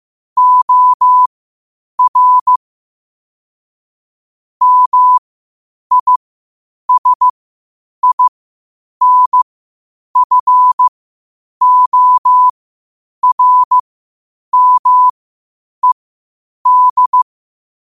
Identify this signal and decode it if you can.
Morse code, heard as a keyed tone, about 15 wpm, spaced as for 8 wpm, 1000 Hz. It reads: OR MISINFORMED
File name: qotdfiles / 8wpm